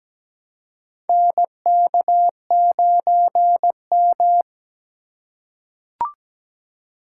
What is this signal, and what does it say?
Morse code, heard as a keyed tone, about 17 wpm, 700 Hz. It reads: NK9M
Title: USA Call Signs - 4x Character Spacing - 17wpm